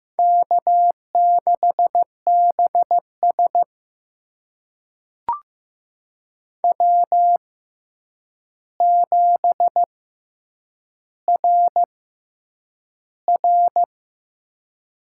Morse code, heard as a keyed tone, about 15 wpm, spaced as for 8 wpm, 700 Hz. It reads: K6BS W7RR
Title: USA Call Signs - 4 Characters - 6x Character Spacing - 15wpm